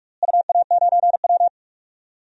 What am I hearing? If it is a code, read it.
Morse code, heard as a keyed tone, about 42 wpm, 700 Hz. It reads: UA9W
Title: DXCC-100 Call Signs - 4 Characters - Repeated 2x - 45wpm